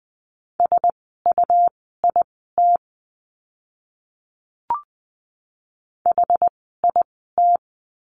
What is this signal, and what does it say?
Morse code, heard as a keyed tone, about 20 wpm, spaced as for 14 wpm, 700 Hz. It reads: SUIT HIT
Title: 101 Lesson 17f - Letter u - Words 20wpm